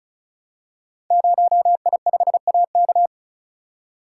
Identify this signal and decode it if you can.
Morse code, heard as a keyed tone, about 35 wpm, 700 Hz. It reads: 0I5AK
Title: Code Groups - 5 Letter:Numbers - 35wpm